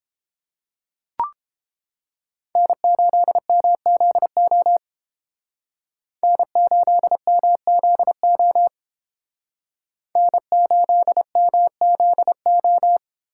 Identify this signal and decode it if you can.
Morse code, heard as a keyed tone, about 30 wpm, 700 Hz. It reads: N8MZO N8MZO N8MZO